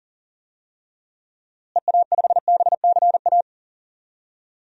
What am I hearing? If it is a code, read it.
Morse code, heard as a keyed tone, about 40 wpm, 700 Hz. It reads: EA5BCA
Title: DXCC-100 Call Signs - 6 Characters - Repeated 2x - 40wpm